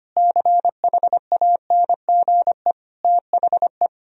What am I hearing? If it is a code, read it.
Morse code, heard as a keyed tone, about 25 wpm, 700 Hz. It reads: CHANGE THE
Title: Sentences from Top 1000 Words - Review Part 1 - 25wpm